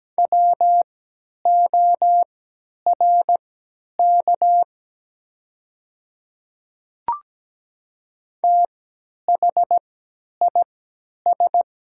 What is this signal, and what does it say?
Morse code, heard as a keyed tone, about 17 wpm, spaced as for 10 wpm, 700 Hz. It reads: WORK THIS
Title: 3x Character Spacing - Top 300 Words - 17wpm